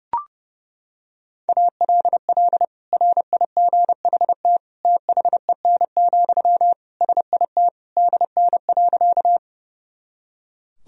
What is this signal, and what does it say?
Morse code, heard as a keyed tone, about 28 wpm, 700 Hz. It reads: ALL RIGHT THEN, SIT DN.
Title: Sentences from Top 500 Words and Abbreviations 30wpm